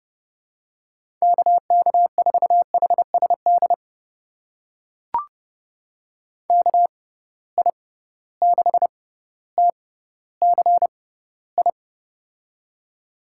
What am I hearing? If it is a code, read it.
Morse code, heard as a keyed tone, about 30 wpm, spaced as for 15 wpm, 700 Hz. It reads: KK4HSD KI6TCI
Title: USA Call Signs - 6 Characters - 6x Character Spacing - 30wpm